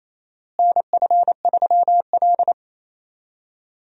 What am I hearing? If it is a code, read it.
Morse code, heard as a keyed tone, about 28 wpm, 700 Hz. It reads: NF3L
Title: USA Call Signs - 4 Characters - Repeated 2x - 28wpm